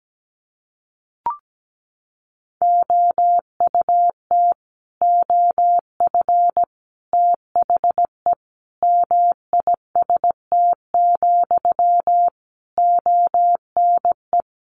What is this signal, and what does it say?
Morse code, heard as a keyed tone, about 17 wpm, 700 Hz. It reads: OUT OF THE MIST, ONE
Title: A Sound Of Thunder 17wpm